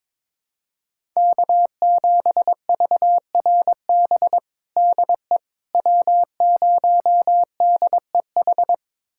Morse code, heard as a keyed tone, about 22 wpm, 700 Hz. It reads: K7VRB DE W0DEH
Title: USA Callsign DE Callsign - 5 Characters - 22wpm